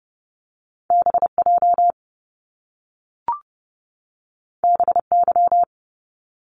Morse code, heard as a keyed tone, about 30 wpm, 700 Hz. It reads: BJ BY
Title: Top Two Letter Combinations in Words 30wpm